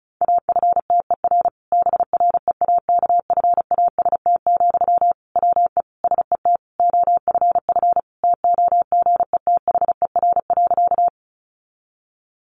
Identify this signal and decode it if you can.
Morse code, heard as a keyed tone, about 34 wpm, 700 Hz. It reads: AFTER BREAKFAST, WE SET OFF TOGETHER.
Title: Sentences from Top 1500 Words 35wpm